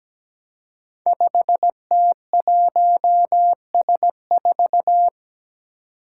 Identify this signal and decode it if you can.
Morse code, heard as a keyed tone, about 17 wpm, 700 Hz. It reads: 5T1S4
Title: Five Challenging Letter-Number Combinations 17wpm